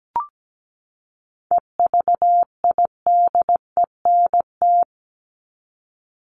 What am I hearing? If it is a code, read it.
Morse code, heard as a keyed tone, about 17 wpm, 700 Hz. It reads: EVIDENT
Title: LY Suffix - Top 100 Words with Base Words 17wpm